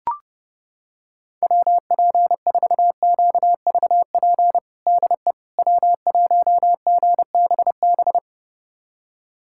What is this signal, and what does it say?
Morse code, heard as a keyed tone, about 32 wpm, 700 Hz. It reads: WP4QVP DE W1GBB
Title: USA Callsign DE Callsign 30wpm